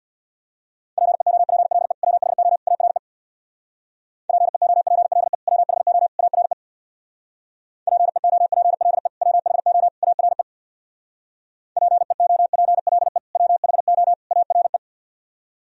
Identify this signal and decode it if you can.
Morse code, heard as a keyed tone, about 66 wpm, 700 Hz. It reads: PEOPLE WHO ARE PEOPLE WHO ARE PEOPLE WHO ARE PEOPLE WHO ARE